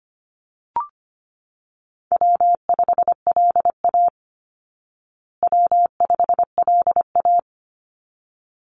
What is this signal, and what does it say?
Morse code, heard as a keyed tone, about 25 wpm, 700 Hz. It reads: W5LA W5LA
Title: Call Signs Repeated 3x - 25wpm